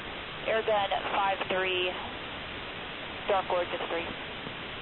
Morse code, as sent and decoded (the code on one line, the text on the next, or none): none